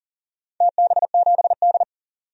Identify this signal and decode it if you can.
Morse code, heard as a keyed tone, about 40 wpm, 700 Hz. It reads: TB7D
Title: DXCC-100 Call Signs - 4 Characters - 40wpm